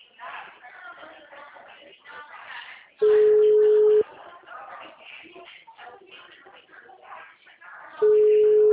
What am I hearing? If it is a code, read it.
Ringback tone (Europe, 425 Hz), with speech.